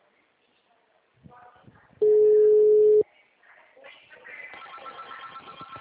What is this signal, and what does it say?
Ringback tone (Europe, 425 Hz), with speech.
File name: out- -2048- - - .51240